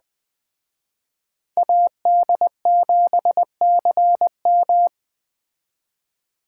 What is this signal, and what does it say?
Morse code, heard as a keyed tone, about 21 wpm, 700 Hz. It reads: AD7CM
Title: Speed Racing - USA 2x2 Call Signs 17wpm